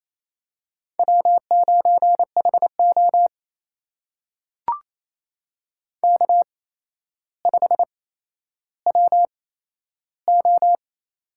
Morse code, heard as a keyed tone, about 27 wpm, spaced as for 12 wpm, 700 Hz. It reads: W9HO K5WO
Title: USA Call Signs - 4 Characters - 8x Character Spacing - 28wpm